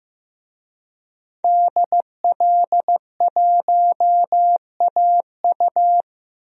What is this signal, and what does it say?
Morse code, heard as a keyed tone, about 16 wpm, 700 Hz. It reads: DL1AU
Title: DXCC-100 Call Signs - 5 Characters - 15wpm